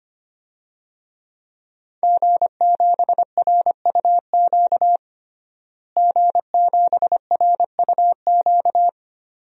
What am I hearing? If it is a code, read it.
Morse code, heard as a keyed tone, about 25 wpm, 700 Hz. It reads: G7RUQ G7RUQ
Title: DXCC-100 Call Signs - 5 Characters - Repeated 2x - 25wpm